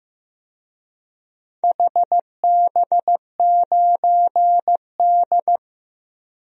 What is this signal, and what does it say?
Morse code, heard as a keyed tone, about 15 wpm, 700 Hz. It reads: HB9D
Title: DXCC-100 Call Signs - 4 Characters - Repeated 2x - 15wpm